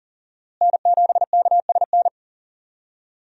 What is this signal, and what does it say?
Morse code, heard as a keyed tone, about 40 wpm, 700 Hz. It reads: N7KSN
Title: USA Call Signs - 5 Characters - Repeated 2x - 40wpm